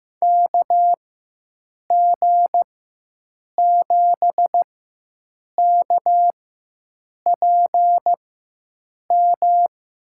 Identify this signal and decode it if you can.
Morse code, heard as a keyed tone, about 15 wpm, spaced as for 7 wpm, 700 Hz. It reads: KG7KPM
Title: USA Call Signs - 6 Characters - 4x Character Spacing - 15wpm